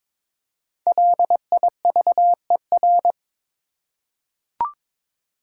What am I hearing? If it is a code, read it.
Morse code, heard as a keyed tone, about 22 wpm, 700 Hz. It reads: LIVER
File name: 5 Letter Words - First 3 Letters Given - 22wpm